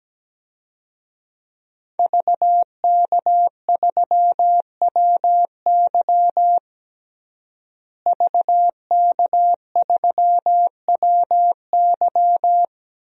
Morse code, heard as a keyed tone, about 17 wpm, 700 Hz. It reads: VK3WY VK3WY
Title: DXCC-100 Call Signs - 5 Characters - Repeated 2x - 17wpm